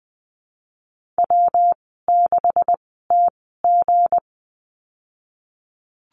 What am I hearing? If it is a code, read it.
Morse code, heard as a keyed tone, about 20 wpm, spaced as for 15 wpm, 700 Hz. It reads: W6TG
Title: 186 Lesson 30g - Letter b - Call Signs 20wpm